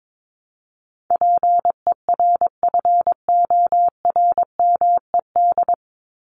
Morse code, heard as a keyed tone, about 22 wpm, 700 Hz. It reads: PERFORMED
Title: ED Suffix - Top 100 Words with Base Words 22wpm